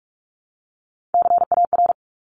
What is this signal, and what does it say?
Morse code, heard as a keyed tone, about 44 wpm, 700 Hz. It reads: CAR
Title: Top 500 Words Repeated 45wpm